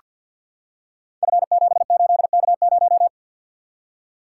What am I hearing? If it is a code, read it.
Morse code, heard as a keyed tone, about 47 wpm, 700 Hz. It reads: F78K0
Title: Code Groups - 5 Letter:Numbers - 50wpm